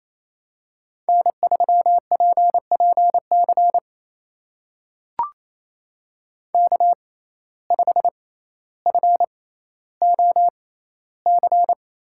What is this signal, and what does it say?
Morse code, heard as a keyed tone, about 28 wpm, spaced as for 14 wpm, 700 Hz. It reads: N3PPC K5FOC
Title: USA Call Signs - 5 Characters - 6x Character Spacing - 28wpm